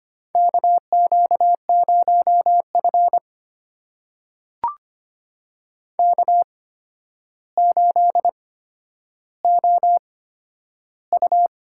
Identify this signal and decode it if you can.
Morse code, heard as a keyed tone, about 25 wpm, spaced as for 11 wpm, 700 Hz. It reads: KQ0F K8OU
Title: USA Call Signs - 4 Characters - 8x Character Spacing - 25wpm